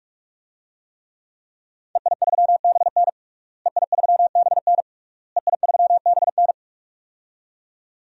Morse code, heard as a keyed tone, about 46 wpm, 700 Hz. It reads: EI3BN EI3BN EI3BN
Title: DXCC-100 Call Signs - 5 Characters - Repeated 3x - 45wpm